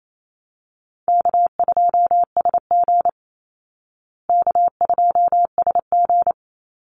Morse code, heard as a keyed tone, about 28 wpm, 700 Hz. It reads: K2SG K2SG
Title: Call Signs Repeated 2x - 28wpm